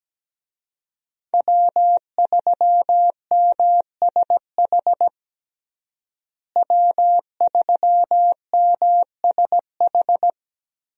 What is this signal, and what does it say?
Morse code, heard as a keyed tone, about 17 wpm, 700 Hz. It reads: W3MSH W3MSH
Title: USA Call Signs - 5 Characters - Repeated 3x - 17wpm